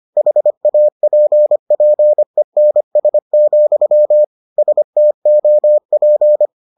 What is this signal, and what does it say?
Morse code, heard as a keyed tone, about 25 wpm, 600 Hz. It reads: HAPPENS, STOP